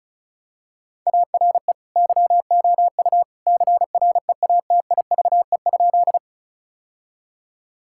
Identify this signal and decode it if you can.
Morse code, heard as a keyed tone, about 36 wpm, 700 Hz. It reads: ARE YOU CREATIVE?
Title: Sentences from Top 1000 Words - Review Part 5 - 35wpm